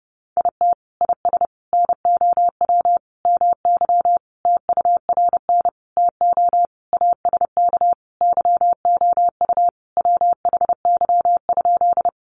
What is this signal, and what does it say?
Morse code, heard as a keyed tone, about 30 wpm, 700 Hz. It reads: IT IS NOW MY TURN TO ASK YOU WHY?
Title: Sentences from Top 200 Words 30wpm